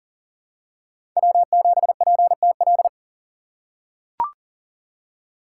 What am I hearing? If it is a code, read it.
Morse code, heard as a keyed tone, about 40 wpm, 700 Hz. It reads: W7PTL
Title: USA Call Signs - 5 Characters - 2x Character Spacing - 40wpm